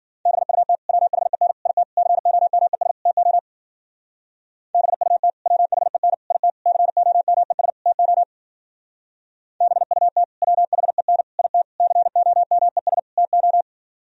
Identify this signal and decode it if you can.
Morse code, heard as a keyed tone, about 51 wpm, 700 Hz. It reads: BUT WHEN IT COMES TO BUT WHEN IT COMES TO BUT WHEN IT COMES TO